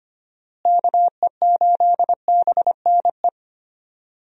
Morse code, heard as a keyed tone, about 25 wpm, 700 Hz. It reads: KE8BNE
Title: USA Call Signs - 6 Characters - Repeated 2x - 25wpm